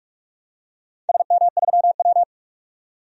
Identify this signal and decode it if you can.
Morse code, heard as a keyed tone, about 44 wpm, 700 Hz. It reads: SM3W
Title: DXCC-100 Call Signs - 4 Characters - 45wpm